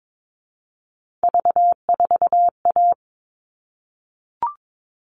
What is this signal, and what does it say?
Morse code, heard as a keyed tone, about 22 wpm, 700 Hz. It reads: V4A
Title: All Cut Numbers - 3-Digit 22wpm